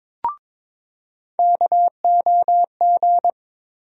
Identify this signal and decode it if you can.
Morse code, heard as a keyed tone, about 21 wpm, 700 Hz. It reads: KOG
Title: USA Call Sign Postfixes - 3 Letters 22wpm